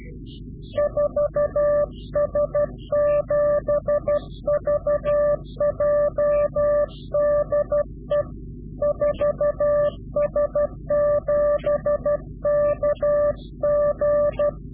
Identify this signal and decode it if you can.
Morse code, heard as a keyed tone, about 12 wpm, 590 Hz. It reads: 4S7VJ DE 4S7KG
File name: CW-QSO / Directional Call / Directional QSO